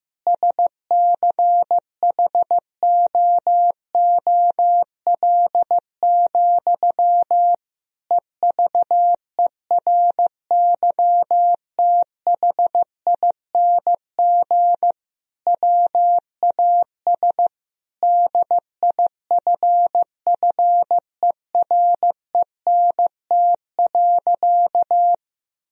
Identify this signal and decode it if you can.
Morse code, heard as a keyed tone, about 15 wpm, 700 Hz. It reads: SCHOOL, EVERYTHING WAS DIFFERENT.
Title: Sentences from Top 1000 Words - Review Part 3 - 15wpm